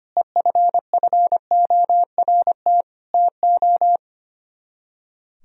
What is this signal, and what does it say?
Morse code, heard as a keyed tone, about 30 wpm, 700 Hz. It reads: EFFORT TO
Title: Speed-Racing - Sets of 4 Words 25wpm